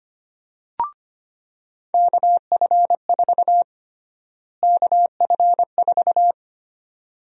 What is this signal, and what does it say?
Morse code, heard as a keyed tone, about 25 wpm, 700 Hz. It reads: KF4 KF4
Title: USA Call Sign Prefixes - Repeated 2x - 25wpm